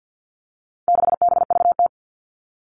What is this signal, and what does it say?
Morse code, heard as a keyed tone, about 50 wpm, 700 Hz. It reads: 6BVT
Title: All Cut Numbers - 4-Digit 50wpm